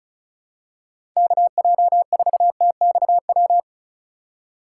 Morse code, heard as a keyed tone, about 36 wpm, 700 Hz. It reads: KJ4TXW
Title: USA Call Signs - 6 Characters - 2x Character Spacing - 35wpm